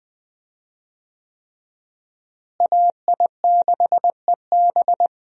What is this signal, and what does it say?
Morse code, heard as a keyed tone, about 20 wpm, 700 Hz. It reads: AI6EB
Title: USA Call Signs - 5 Characters - Repeated 3x - 20wpm